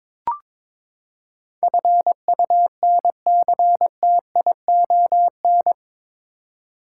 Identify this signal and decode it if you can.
Morse code, heard as a keyed tone, about 22 wpm, 700 Hz. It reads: FUNCTION
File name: practice / 22wpm / Eight Letter Words - 22wpm